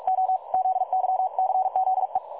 Morse code, heard as a keyed tone, about 44 wpm, 750 Hz. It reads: O ZYCZE
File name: dlazaawansowanych / nagraniazpasma / websdr recording 3565.0kHz